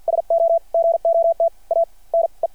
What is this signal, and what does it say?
Morse code, heard as a keyed tone, about 50 wpm, spaced as for 42 wpm, 660 Hz. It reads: SO GOT A NE
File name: archived / CRSnet